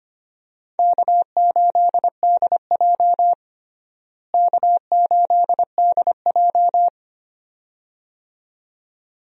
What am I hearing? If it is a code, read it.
Morse code, heard as a keyed tone, about 25 wpm, 700 Hz. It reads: K8DJ K8DJ
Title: USA Call Signs - 4 Characters - Repeated 2x - 25wpm